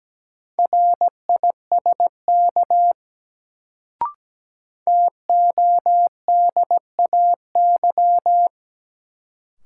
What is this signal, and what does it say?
Morse code, heard as a keyed tone, about 17 wpm, 700 Hz. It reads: RISK TODAY
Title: Sentences from Top 600 Words - New Word Drill - 17wpm